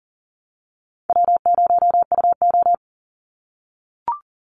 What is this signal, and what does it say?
Morse code, heard as a keyed tone, about 41 wpm, 700 Hz. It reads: W0UO
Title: Call Signs - 40wpm